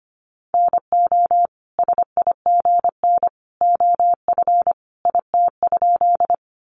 Morse code, heard as a keyed tone, about 25 wpm, 700 Hz. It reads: NO SIGN OF IT?
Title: Sentences from Top 1000 Words 25wpm